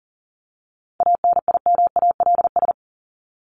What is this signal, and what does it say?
Morse code, heard as a keyed tone, about 40 wpm, 700 Hz. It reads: ANIMALS
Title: ES:S Suffix - Top 100 Words with Base Words 40wpm